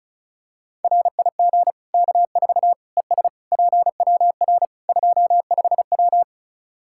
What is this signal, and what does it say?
Morse code, heard as a keyed tone, about 36 wpm, 700 Hz. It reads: RIG K4 ES PWR 25W
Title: Practice QSO - Exchange 2 - Info - 35wpm